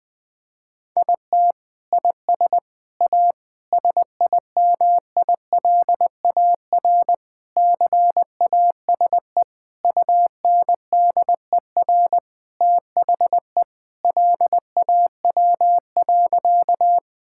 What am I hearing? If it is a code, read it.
Morse code, heard as a keyed tone, about 19 wpm, 700 Hz. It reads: IT IS A SIMILAR CASE UNDER THE LAW.
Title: Sentences from Top 1000 Words - Review Part 5 - 20wpm